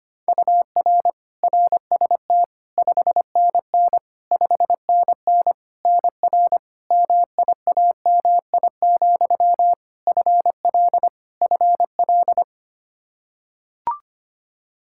Morse code, heard as a keyed tone, about 25 wpm, 700 Hz. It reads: UR RST 5NN 5NN NR MIAMI, FL FL
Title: K1USN SST QSO Practice 25wpm